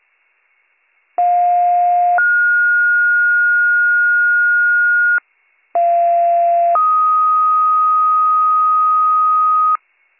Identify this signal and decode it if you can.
Two-tone page, heard as a one-second tone, then a three-second tone, about 710 then 1430 Hz, about 690 then 1250 Hz. It is heard from 1.2 to 9.8 s.